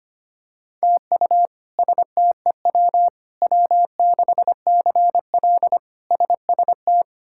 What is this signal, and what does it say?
Morse code, heard as a keyed tone, about 25 wpm, 700 Hz. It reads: TU STEW W6CL SST
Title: K1USN SST QSO Practice - 25wpm